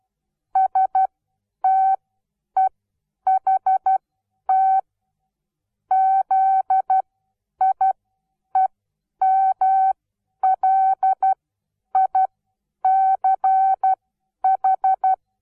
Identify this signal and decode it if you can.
Morse code, heard as a keyed tone, about 12 wpm, spaced as for 9 wpm, 760 Hz. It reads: STEHT ZIEMLICH
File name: Onlinelehrgang / morsen / DJ4UF Morsen17